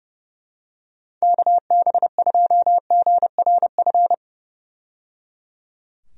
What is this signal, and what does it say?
Morse code, heard as a keyed tone, about 30 wpm, 700 Hz. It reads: KB2GRF
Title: USA Call Signs - 6 Characters - 30wpm